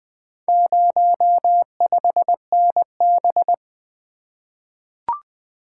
Morse code, heard as a keyed tone, about 20 wpm, 700 Hz. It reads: 05NB
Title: Four Letter-Number Combinations - 20wpm